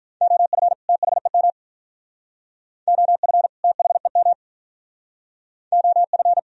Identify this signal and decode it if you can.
Morse code, heard as a keyed tone, about 43 wpm, 700 Hz. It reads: OF THEM OF THEM OF
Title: Speed-Racing - Sets of 2 Words 35wpm